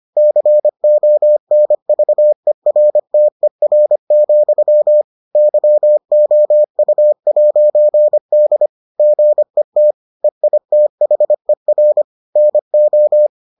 Morse code, heard as a keyed tone, about 25 wpm, 600 Hz. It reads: CONVERTER, YOU'D GET EITHER NO